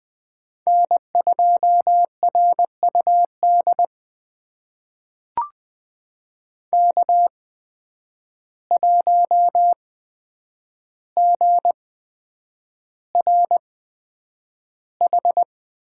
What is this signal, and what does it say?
Morse code, heard as a keyed tone, about 21 wpm, spaced as for 9 wpm, 700 Hz. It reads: N2RUD K1GRH
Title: USA Call Signs - 8x Character Spacing - 20wpm